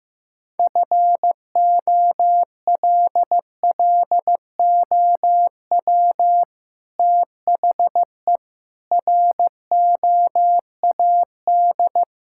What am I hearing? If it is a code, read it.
Morse code, heard as a keyed tone, about 15 wpm, 700 Hz. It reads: FOLLOW THE ROAD
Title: Sentences from Top 1000 Words - Review Part 6 - 15wpm